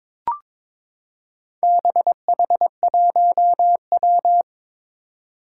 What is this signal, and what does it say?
Morse code, heard as a keyed tone, about 22 wpm, 700 Hz. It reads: BH1W
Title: Four Challenging Letter-Number Combinations 22wpm